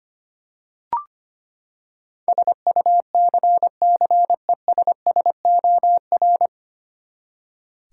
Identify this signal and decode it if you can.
Morse code, heard as a keyed tone, about 25 wpm, 700 Hz. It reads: SUCCESSOR
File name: Nine Letter Words - 25wpm